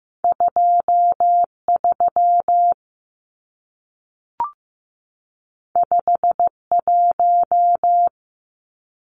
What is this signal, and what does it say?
Morse code, heard as a keyed tone, about 15 wpm, 700 Hz. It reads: 23 51
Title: Numbers - 2-Digit 15wpm